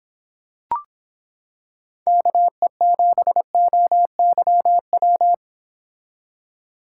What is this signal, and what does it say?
Morse code, heard as a keyed tone, about 26 wpm, 700 Hz. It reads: KE7OYW